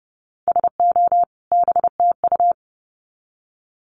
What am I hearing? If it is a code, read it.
Morse code, heard as a keyed tone, about 30 wpm, 700 Hz. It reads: SO BTU
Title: Common QSO Phrases 30wpm